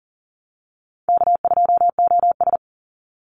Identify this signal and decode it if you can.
Morse code, heard as a keyed tone, about 40 wpm, 700 Hz. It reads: K2OS
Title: Call Signs Repeated 2x - 40wpm